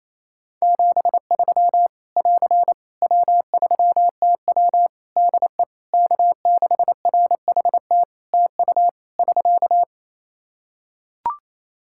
Morse code, heard as a keyed tone, about 28 wpm, 700 Hz. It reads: 73 <AR> W3TW DE K6RHT TU <SK>
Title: Practice QSO - Exchange 4 - Ending - 28wpm